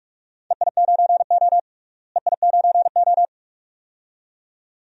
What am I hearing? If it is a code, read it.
Morse code, heard as a keyed tone, about 46 wpm, 700 Hz. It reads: EI9O EI9O
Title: DXCC-100 Call Signs - 4 Characters - Repeated 2x - 45wpm